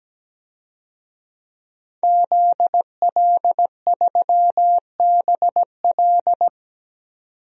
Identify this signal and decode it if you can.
Morse code, heard as a keyed tone, about 17 wpm, 700 Hz. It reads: ZL3BL
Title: DXCC-100 Call Signs - 5 Characters - 17wpm